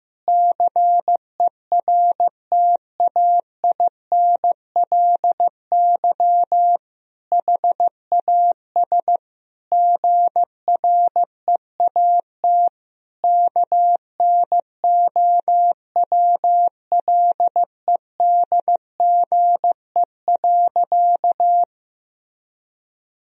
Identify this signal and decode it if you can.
Morse code, heard as a keyed tone, about 15 wpm, 700 Hz. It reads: CERTAINLY HAS GREAT KNOWLEDGE.
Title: Sentences from Top 1000 Words - Review Part 3 - 15wpm